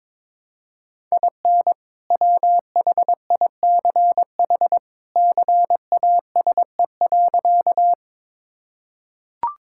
Morse code, heard as a keyed tone, about 22 wpm, 700 Hz. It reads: IN WHICH CASE.
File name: practice / 22wpm / Sentences with 2-Word Phrases - 22wpm